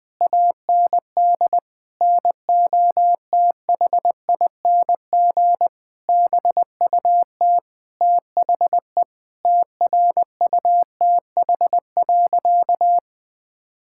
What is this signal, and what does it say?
Morse code, heard as a keyed tone, about 21 wpm, 700 Hz. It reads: AND NOTHING BUT THE TRUTH.
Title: Sentences with 9-Word Phrases - 20wpm